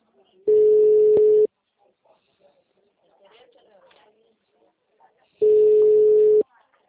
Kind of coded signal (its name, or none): telephone call progress tone